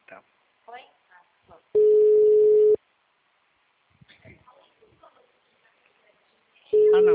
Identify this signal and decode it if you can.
Ringback tone (Europe, 425 Hz), with speech.